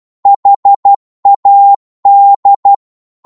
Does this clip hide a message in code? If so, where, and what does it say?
Yes: Morse code from 0.3 to 2.7 s, heard as a keyed tone, about 12 wpm, 800 Hz. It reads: HAD